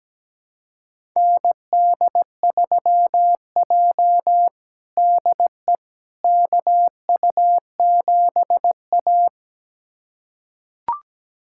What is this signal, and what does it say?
Morse code, heard as a keyed tone, about 17 wpm, 700 Hz. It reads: ND3J DE KU7A
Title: USA Callsign DE Callsign - 4 Characters - 17wpm